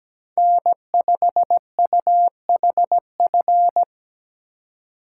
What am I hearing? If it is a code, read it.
Morse code, heard as a keyed tone, about 20 wpm, 700 Hz. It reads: N5UHF
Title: Speed Racing - USA 1x3 Call Signs 17wpm